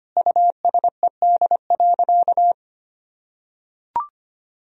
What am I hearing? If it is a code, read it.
Morse code, heard as a keyed tone, about 25 wpm, 700 Hz. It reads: USED.
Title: Sentences from Top 1000 Words - Review Part 1 - 25wpm